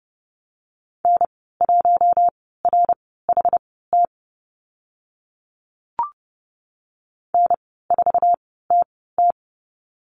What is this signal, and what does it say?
Morse code, heard as a keyed tone, about 30 wpm, spaced as for 17 wpm, 700 Hz. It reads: N1RHT N4TT
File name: Lesson 14g - Number 2 - Call Signs 30wpm